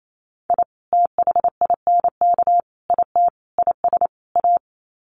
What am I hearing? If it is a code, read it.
Morse code, heard as a keyed tone, about 28 wpm, 700 Hz. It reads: I THINK IT IS A
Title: Sets of 5 Words Encore 28wpm